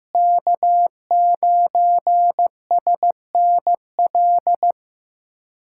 Morse code, heard as a keyed tone, about 16 wpm, 700 Hz. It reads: K9SNL
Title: USA Callsign DE Callsign - 5 Characters - 15wpm